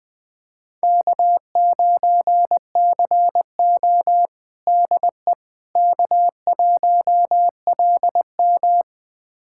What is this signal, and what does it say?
Morse code, heard as a keyed tone, about 20 wpm, 700 Hz. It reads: K9CO DE K1LM
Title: USA Callsign DE Callsign - 4 Characters - 20wpm